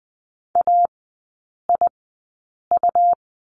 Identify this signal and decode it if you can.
Morse code, heard as a keyed tone, about 19 wpm, spaced as for 8 wpm, 700 Hz. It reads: AIU
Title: 100 Lesson 17e - Letter u - Three Characters 20wpm